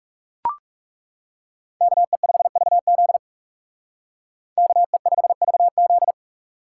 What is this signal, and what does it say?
Morse code, heard as a keyed tone, about 39 wpm, 700 Hz. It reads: KE5VZ KE5VZ